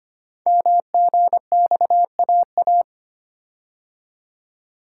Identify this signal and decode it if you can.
Morse code, heard as a keyed tone, about 25 wpm, 700 Hz. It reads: MGXAA
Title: Code Groups - 5 Letters - 25wpm